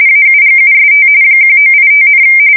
RTTY, about 45.45 baud. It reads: AUGUST 1985,
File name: fsk170 / USS-New-Jersey